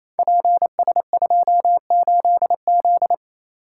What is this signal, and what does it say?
Morse code, heard as a keyed tone, about 29 wpm, 700 Hz. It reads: PS28Z
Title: Code Groups - 5 Letter:Numbers - 28wpm